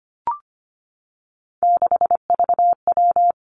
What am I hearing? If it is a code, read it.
Morse code, heard as a keyed tone, about 25 wpm, 700 Hz. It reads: 6VW
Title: Three Challenging Letter-Number Combinations 25wpm